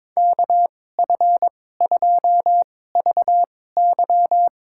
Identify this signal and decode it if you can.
Morse code, heard as a keyed tone, about 22 wpm, spaced as for 16 wpm, 700 Hz. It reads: KF2VY
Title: USA Call Signs - 5 Characters - 2x Character Spacing - 22wpm